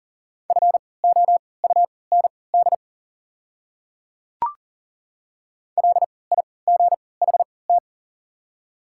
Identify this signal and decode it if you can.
Morse code, heard as a keyed tone, about 40 wpm, spaced as for 23 wpm, 700 Hz. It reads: FOUND LIGHT
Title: 3x Character Spacing - Top 500 Words - 40wpm